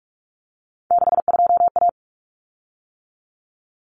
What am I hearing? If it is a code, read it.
Morse code, heard as a keyed tone, about 44 wpm, 700 Hz. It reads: 62A
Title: Common Cut Numbers - 3-Digit 45wpm